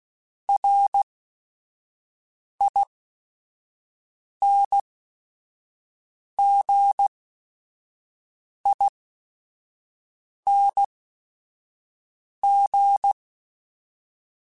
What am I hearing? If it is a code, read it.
Morse code, heard as a keyed tone, about 16 wpm, spaced as for 5 wpm, 800 Hz. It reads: RINGING